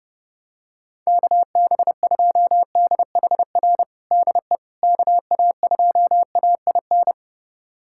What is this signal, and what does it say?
Morse code, heard as a keyed tone, about 30 wpm, 700 Hz. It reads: KB2DHR DE KA2AIN
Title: USA Callsign DE Callsign - 6 Characters - 30wpm